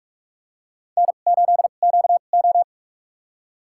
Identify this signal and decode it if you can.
Morse code, heard as a keyed tone, about 47 wpm, spaced as for 33 wpm, 700 Hz. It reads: N8QO
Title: USA Call Signs - 4 Characters - 2x Character Spacing - 45wpm